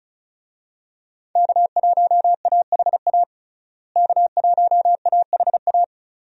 Morse code, heard as a keyed tone, about 36 wpm, 700 Hz. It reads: K1AHA K1AHA
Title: USA Call Signs - 5 Characters - Repeated 3x - 35wpm